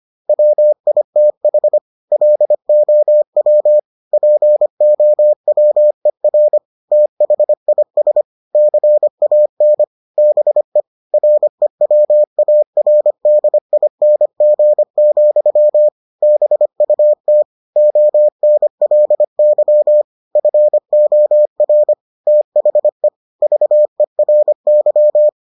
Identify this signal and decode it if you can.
Morse code, heard as a keyed tone, about 25 wpm, 600 Hz. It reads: WITH LOW POWER THIS CAN BE REWARDING, BUT ONLY FOR THE VERY